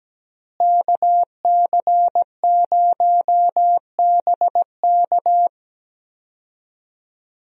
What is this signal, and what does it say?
Morse code, heard as a keyed tone, about 17 wpm, 700 Hz. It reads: KC0BK
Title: USA Call Signs - 5 Characters - Repeated 3x - 17wpm